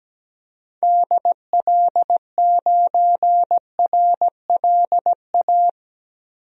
Morse code, heard as a keyed tone, about 17 wpm, 700 Hz. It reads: DL9RLA
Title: DXCC-100 Call Signs - 6 Characters - Repeated 3x - 17wpm